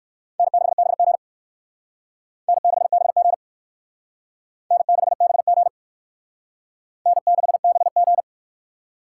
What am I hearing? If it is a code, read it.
Morse code, heard as a keyed tone, about 56 wpm, 700 Hz. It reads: N6BG N6BG N6BG N6BG
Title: Speed Racing - USA 1x2 Call Signs 45wpm